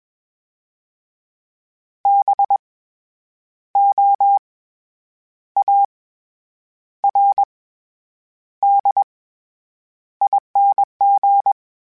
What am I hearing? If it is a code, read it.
Morse code, heard as a keyed tone, about 21 wpm, 800 Hz. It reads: B O A R D ING
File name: Special INGs-mp3s / Special INGs-mp3s / SlowToFaster ing @20